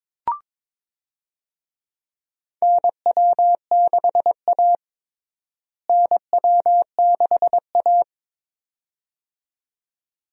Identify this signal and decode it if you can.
Morse code, heard as a keyed tone, about 22 wpm, 700 Hz. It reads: NW6A NW6A
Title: USA Call Signs - 4 Characters - Repeated 2x - 22wpm